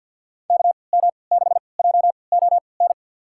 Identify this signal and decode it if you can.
Morse code, heard as a keyed tone, about 46 wpm, spaced as for 27 wpm, 700 Hz. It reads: KM6JON
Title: USA Call Signs - 6 Characters - 3x Character Spacing - 50wpm